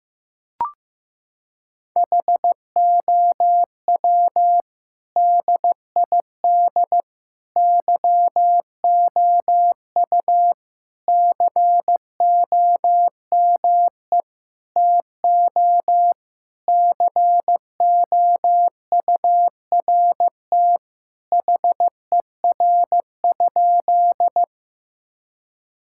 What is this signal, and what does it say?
Morse code, heard as a keyed tone, about 15 wpm, 700 Hz. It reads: HOW DID YOU COME TO COURT HER?
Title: Sentences from Top 1600 Words 15wpm